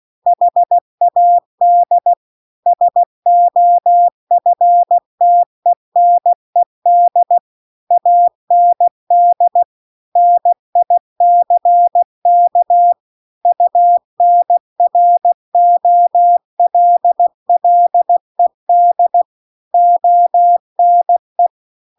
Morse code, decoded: HAD SOFTENED AND NICK UNROLLED ONE

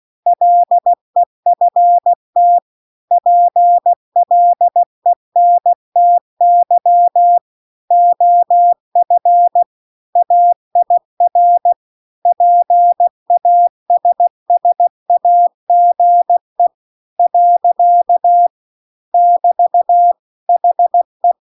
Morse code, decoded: LEFT PLENTY OF AIR PASSAGE . <BT> HE

